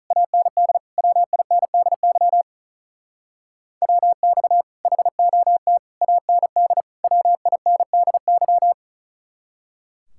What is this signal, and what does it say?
Morse code, heard as a keyed tone, about 44 wpm, 700 Hz. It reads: AND WINDY WX HOT AND WINDY